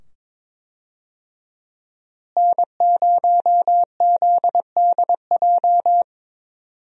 Morse code, heard as a keyed tone, about 22 wpm, 700 Hz. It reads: N0ZDJ
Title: USA Call Signs - 5 Characters - Repeated 2x - 22wpm